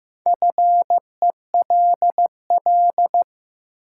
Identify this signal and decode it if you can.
Morse code, heard as a keyed tone, about 15 wpm, 700 Hz. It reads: FELL
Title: Sets of 5 Words Encore - New Word Drill - 15wpm